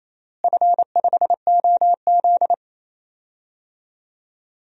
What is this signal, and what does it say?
Morse code, heard as a keyed tone, about 28 wpm, 700 Hz. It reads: F5OZ
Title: DXCC-100 Call Signs - 4 Characters - Repeated 2x - 28wpm